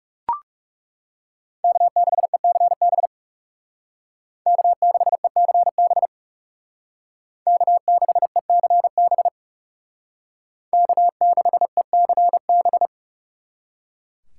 Morse code, heard as a keyed tone, about 37 wpm, 700 Hz. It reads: K6ECB K6ECB K6ECB K6ECB
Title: Speed Racing - USA 1x3 Call Signs 30wpm